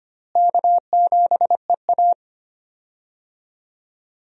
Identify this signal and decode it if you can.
Morse code, heard as a keyed tone, about 25 wpm, 700 Hz. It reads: K7EA
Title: USA Call Signs - 4 Characters - Repeated 3x - 25wpm